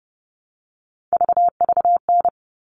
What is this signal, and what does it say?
Morse code, heard as a keyed tone, about 30 wpm, 700 Hz. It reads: VVN
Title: All Cut Numbers - 3-Digit 30wpm